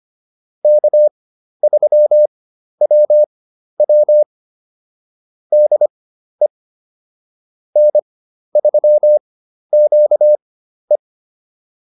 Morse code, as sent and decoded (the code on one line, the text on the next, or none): -.- ...-- .-- .-- / -.. / -. ...-- --.- .
K3WW DE N3QE